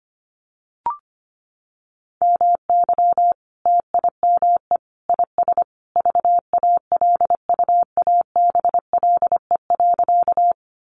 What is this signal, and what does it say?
Morse code, heard as a keyed tone, about 25 wpm, 700 Hz. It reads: MY TIME IS VALUABLE.
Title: Sentences from Top 1100 Words 25wpm